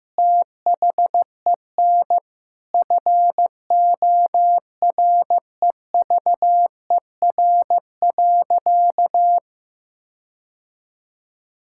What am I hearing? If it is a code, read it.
Morse code, heard as a keyed tone, about 15 wpm, 700 Hz. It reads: THEN FOREVER.
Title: Sentences from Top 1600 Words 15wpm